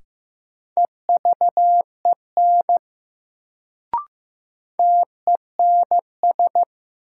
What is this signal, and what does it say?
Morse code, heard as a keyed tone, about 15 wpm, 700 Hz. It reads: EVEN TENS
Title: Sets of 4 Words - New Word Drill - 15wpm